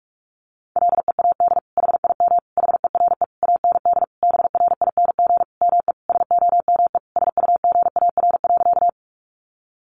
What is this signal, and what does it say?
Morse code, heard as a keyed tone, about 44 wpm, 700 Hz. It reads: LEAD HIM HERE AND BRING ME SOME SUGAR.
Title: Sentences from Top 1200 Words 45wpm